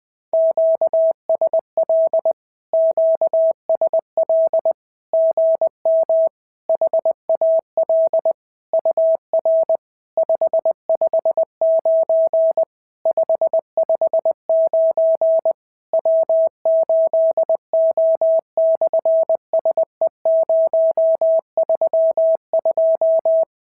Morse code, heard as a keyed tone, about 20 wpm, 690 Hz. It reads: QSL QSL GM HAL UR 559 559 W8O/SE032
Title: SOTA QSO Practice 20wpm